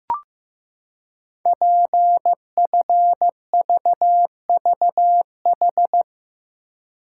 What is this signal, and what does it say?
Morse code, heard as a keyed tone, about 15 wpm, 700 Hz. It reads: PFVVH
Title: Code Groups - 5 Letters - 15wpm